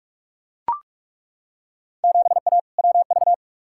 Morse code, heard as a keyed tone, about 42 wpm, 700 Hz. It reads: 7A WV
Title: Field Day Exchanges Encore - 45wpm